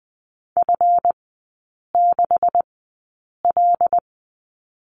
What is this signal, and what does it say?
Morse code, heard as a keyed tone, about 20 wpm, spaced as for 8 wpm, 700 Hz. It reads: F6L
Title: 145 Lesson 24e - Letter y - Three Characters 20wpm